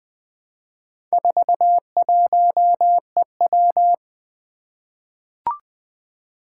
Morse code, heard as a keyed tone, about 20 wpm, 700 Hz. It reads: 41EW
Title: Four Challenging Letter-Number Combinations 20wpm